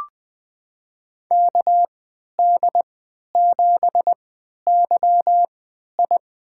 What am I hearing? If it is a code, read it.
Morse code, heard as a keyed tone, about 21 wpm, spaced as for 11 wpm, 700 Hz. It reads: KD7YI
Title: USA Call Signs - 3x Character Spacing - 20wpm